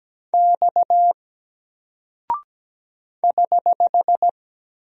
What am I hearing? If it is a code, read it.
Morse code, heard as a keyed tone, about 17 wpm, 700 Hz. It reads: X <HH>
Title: Single Character (Letter:Number:Punctuation:Prosign) 17wpm